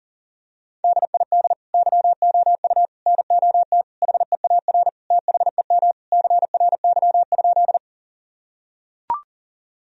Morse code, heard as a keyed tone, about 40 wpm, 700 Hz. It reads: DID YOU NOT HEAR THEM CRY?
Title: Sentences from Top 1000 Words - Review Part 2 - 40wpm